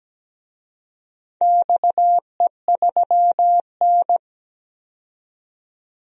Morse code, heard as a keyed tone, about 17 wpm, 700 Hz. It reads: XE3N
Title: DXCC-100 Call Signs - 4 Characters - 17wpm